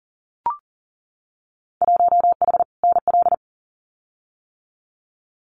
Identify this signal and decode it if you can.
Morse code, heard as a keyed tone, about 41 wpm, 700 Hz. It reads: 1H NL
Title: Winter Field Day Exchanges 40wpm